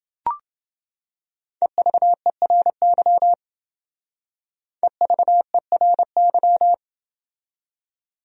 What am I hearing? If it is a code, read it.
Morse code, heard as a keyed tone, about 28 wpm, 700 Hz. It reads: EVERY EVERY